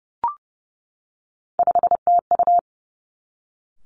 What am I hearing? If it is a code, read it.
Morse code, heard as a keyed tone, about 30 wpm, 700 Hz. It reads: 5TU
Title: Three Challenging Letter-Number Combinations 30wpm